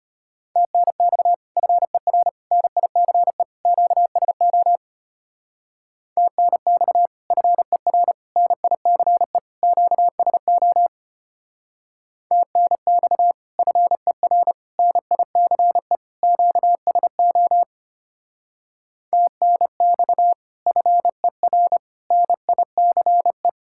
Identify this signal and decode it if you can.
Morse code, heard as a keyed tone, about 29 wpm, 700 Hz. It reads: TNX FER NICE QSO TNX FER NICE QSO TNX FER NICE QSO TNX FER NICE
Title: Speed-Racing Common QSO Phrases Encore - 25wpm